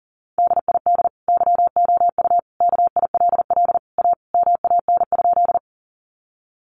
Morse code, heard as a keyed tone, about 40 wpm, 700 Hz. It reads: DID YOU KILL A MAN?
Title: Sentences from Top 1400 Words 40wpm